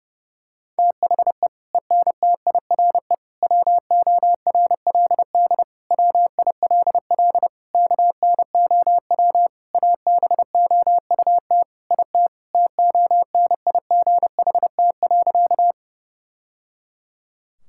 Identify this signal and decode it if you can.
Morse code, heard as a keyed tone, about 30 wpm, 700 Hz. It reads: THE ENTIRE WORLD WILL KNOW ABOUT IT TONIGHT.
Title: Sentences from Top 1000 Words - Review Part 5 - 30wpm